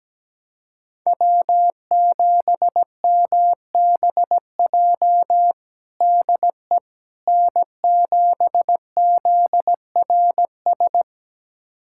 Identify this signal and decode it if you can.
Morse code, heard as a keyed tone, about 17 wpm, 700 Hz. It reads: W7MBJ DE N7ZRS
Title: USA Callsign DE Callsign - 5 Characters - 17wpm